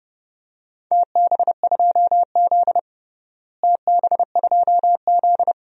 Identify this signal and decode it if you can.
Morse code, heard as a keyed tone, about 30 wpm, 700 Hz. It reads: TB2Z TB2Z
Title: DXCC-100 Call Signs - 4 Characters - Repeated 2x - 30wpm